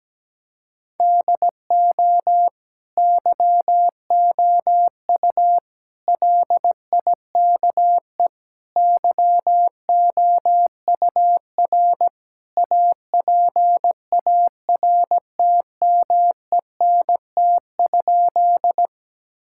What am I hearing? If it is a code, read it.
Morse code, heard as a keyed tone, about 17 wpm, 700 Hz. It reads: DO YOU LIKE YOUR APARTMENT?
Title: Sentences from Top 1000 Words - Review Part 6 - 17wpm